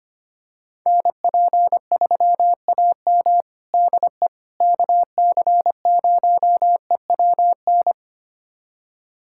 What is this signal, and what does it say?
Morse code, heard as a keyed tone, about 25 wpm, 700 Hz. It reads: NP3AM DE KC0EWN
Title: USA Callsign DE Callsign 25wpm